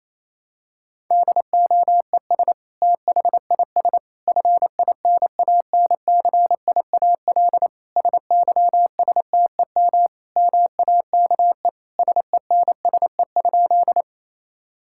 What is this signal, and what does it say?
Morse code, heard as a keyed tone, about 28 wpm, 700 Hz. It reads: DOES THIS FINANCIAL SYSTEM MAKE SENSE?
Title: Sentences from Top 1000 Words - Review Part 5 - 28wpm